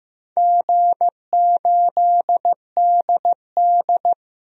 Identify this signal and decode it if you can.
Morse code, heard as a keyed tone, about 15 wpm, 700 Hz. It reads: G8DD
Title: DXCC-100 Call Signs - 4 Characters - 15wpm